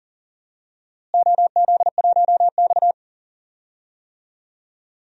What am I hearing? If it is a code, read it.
Morse code, heard as a keyed tone, about 40 wpm, 700 Hz. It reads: OZ1X
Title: DXCC-100 Call Signs - 4 Characters - Repeated 2x - 40wpm